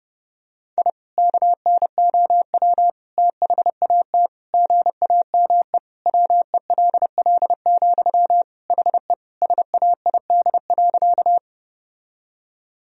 Morse code, heard as a keyed tone, about 32 wpm, 700 Hz. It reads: I KNOW THAT GAME WELL, HE SAID.
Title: Sentences from Top 1000 Words - Review Part 3 - 30wpm